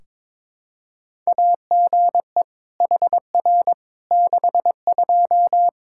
Morse code, heard as a keyed tone, about 22 wpm, 700 Hz. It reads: AGE HR 62
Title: Practice QSO - Exchange 3 - More Info - 22wpm